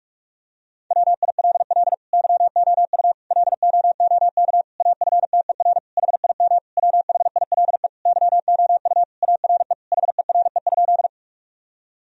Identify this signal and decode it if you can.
Morse code, heard as a keyed tone, about 48 wpm, 700 Hz. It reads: WILL YOU LOOK AFTER HIM WHILE YOU ARE HERE?
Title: Sentences from Top 1000 Words - Review Part 1 - 45wpm